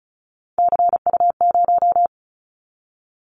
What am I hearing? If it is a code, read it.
Morse code, heard as a keyed tone, about 34 wpm, 700 Hz. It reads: CU0
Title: Callsign Prefixes for Top 50 DXCC - 35wpm